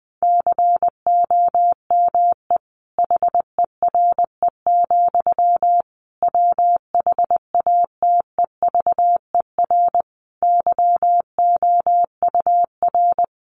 Morse code, heard as a keyed tone, about 19 wpm, 700 Hz. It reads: COME HERE, WHATEVER YOUR
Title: Sentences from Top 1100 Words 20wpm